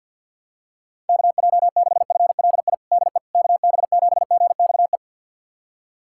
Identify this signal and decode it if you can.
Morse code, heard as a keyed tone, about 52 wpm, 700 Hz. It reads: KJ6FLI DE KB7GXE
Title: USA Callsign DE Callsign - 6 Characters - 50wpm